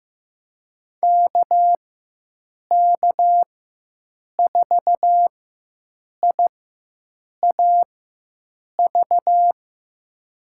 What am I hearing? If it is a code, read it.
Morse code, heard as a keyed tone, about 15 wpm, spaced as for 7 wpm, 700 Hz. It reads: KK4IAV
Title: USA Call Signs - 6 Characters - 4x Character Spacing - 15wpm